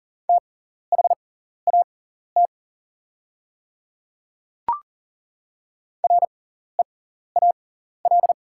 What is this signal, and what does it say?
Morse code, heard as a keyed tone, about 40 wpm, spaced as for 14 wpm, 700 Hz. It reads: THAT REAL
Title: 6x Character Spacing - Top 500 Words - 40wpm